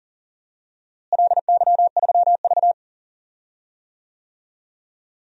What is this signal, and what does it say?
Morse code, heard as a keyed tone, about 40 wpm, 700 Hz. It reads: LY3V
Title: DXCC-100 Call Signs - 4 Characters - Repeated 2x - 40wpm